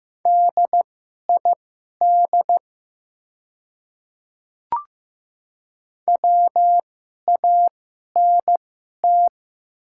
Morse code, heard as a keyed tone, about 15 wpm, spaced as for 11 wpm, 700 Hz. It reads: DID WANT
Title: 2x Character Spacing - Top 200 Words - 15wpm